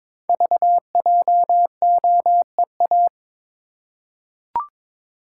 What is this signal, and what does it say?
Morse code, heard as a keyed tone, about 22 wpm, 700 Hz. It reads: VJOEA
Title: Code Groups - 5 Letters - 22wpm